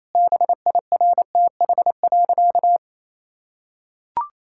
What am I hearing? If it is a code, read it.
Morse code, heard as a keyed tone, about 28 wpm, 700 Hz. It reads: BIRTH.
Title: Sentences from Top 1000 Words - Review Part 5 - 28wpm